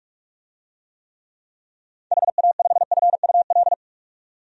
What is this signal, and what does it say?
Morse code, heard as a keyed tone, about 41 wpm, 700 Hz. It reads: HA5FUL
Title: DXCC-100 Call Signs - 6 Characters - 45wpm